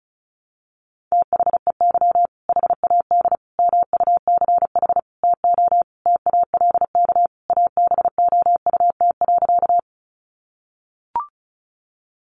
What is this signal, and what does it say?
Morse code, heard as a keyed tone, about 34 wpm, 700 Hz. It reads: THEY HAD MUCH TO TALK ABOUT.
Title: Sentences from Top 400 Words 35wpm